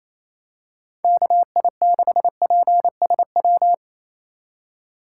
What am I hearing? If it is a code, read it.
Morse code, heard as a keyed tone, about 28 wpm, 700 Hz. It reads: KI6PSW
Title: USA Call Signs - 3x Character Spacing - 28wpm